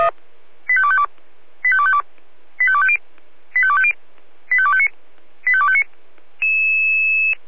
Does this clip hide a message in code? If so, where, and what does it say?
Yes: five-tone selective call (ZVEI) from 0.7 to 5.8 s, heard as quick runs of five tones, tones of about 70 ms. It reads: 85252 85252 85250 85250 85259 85259